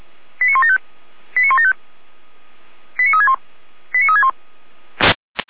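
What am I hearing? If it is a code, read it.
Five-tone selective call (ZVEI), heard as quick runs of five tones, tones of about 70 ms. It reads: 89175 89175 89361 89361
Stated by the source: ZVEI five-tone call received via SDR.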